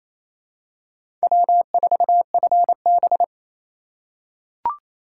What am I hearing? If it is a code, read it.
Morse code, heard as a keyed tone, about 28 wpm, 700 Hz. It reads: W4FB
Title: USA Call Signs - 4 Characters - 4x Character Spacing - 28wpm